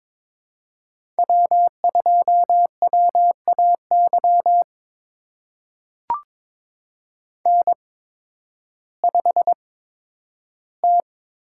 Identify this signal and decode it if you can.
Morse code, heard as a keyed tone, about 21 wpm, spaced as for 12 wpm, 700 Hz. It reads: W2WAY N5T
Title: USA Call Signs - 8x Character Spacing - 22wpm